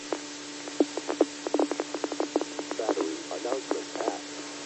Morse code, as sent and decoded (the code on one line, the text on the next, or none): none